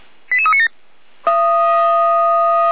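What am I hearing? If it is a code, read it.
Five-tone selective call (ZVEI), heard as quick runs of five tones, tones of about 70 ms. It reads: 88397
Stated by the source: ZVEI five-tone selective call received by SDR on channel 455.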